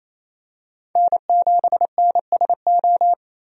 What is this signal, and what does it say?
Morse code, heard as a keyed tone, about 28 wpm, 700 Hz. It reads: N7NSO
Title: USA Call Signs - 5 Characters - 28wpm